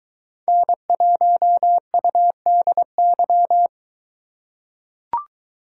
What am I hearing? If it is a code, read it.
Morse code, heard as a keyed tone, about 21 wpm, 700 Hz. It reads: N1UDY